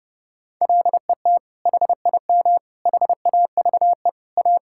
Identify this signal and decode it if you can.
Morse code, heard as a keyed tone, about 32 wpm, 700 Hz. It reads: LET HIM HAVE A
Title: Sentences from Top 1600 Words 30wpm